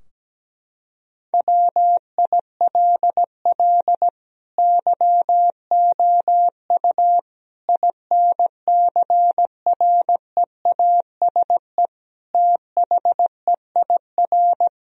Morse code, heard as a keyed tone, about 17 wpm, 700 Hz. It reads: WILL YOU INCREASE THEIR
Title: Sentences from Top 1000 Words - Review Part 4 - 17wpm